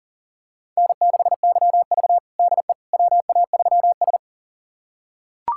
Morse code, heard as a keyed tone, about 40 wpm, 700 Hz. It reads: N6YV DE WA3S
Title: USA Callsign DE Callsign - 4 Characters - 40wpm